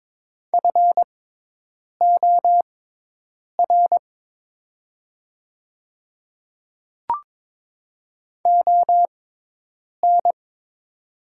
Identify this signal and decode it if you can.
Morse code, heard as a keyed tone, about 22 wpm, spaced as for 8 wpm, 700 Hz. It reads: FOR ON
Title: 6x Character Spacing - Top 100 Words - 22wpm